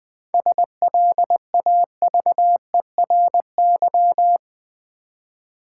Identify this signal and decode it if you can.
Morse code, heard as a keyed tone, about 20 wpm, 700 Hz. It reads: SLAVERY
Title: 7 Letter Words - First 3 Letters Given - 20wpm